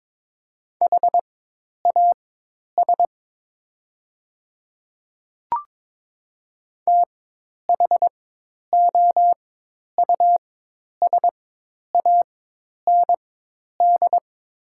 Morse code, heard as a keyed tone, about 22 wpm, spaced as for 10 wpm, 700 Hz. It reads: HAS THOUSAND
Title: 4x Character Spacing - Top 500 Words - 22wpm